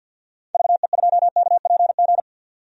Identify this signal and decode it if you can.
Morse code, heard as a keyed tone, about 51 wpm, 700 Hz. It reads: VE2KPG
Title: DXCC-100 Call Signs - 6 Characters - 50wpm